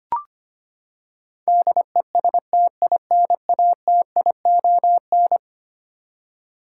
Morse code, heard as a keyed tone, about 25 wpm, 700 Hz. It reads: DESTINATION
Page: morse-code-ninja-practice-25wpm